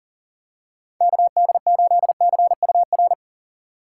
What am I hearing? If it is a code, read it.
Morse code, heard as a keyed tone, about 40 wpm, 700 Hz. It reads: KD8CUR
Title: USA Call Signs - 6 Characters - 4x Character Spacing - 40wpm